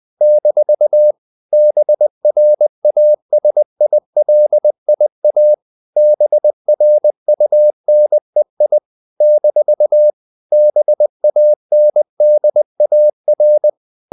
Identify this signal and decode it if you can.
Morse code, heard as a keyed tone, about 20 wpm, 600 Hz. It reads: - BRASILIA BRUNEI - BANDAR
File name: CW / Practice / NationalCapitals20